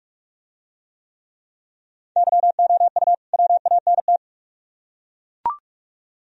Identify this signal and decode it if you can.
Morse code, heard as a keyed tone, about 42 wpm, 700 Hz. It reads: YOU WANT